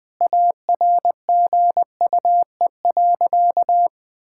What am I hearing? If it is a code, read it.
Morse code, heard as a keyed tone, about 20 wpm, 700 Hz. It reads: ARGUE.
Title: Sentences with 7-Word Phrases - 20wpm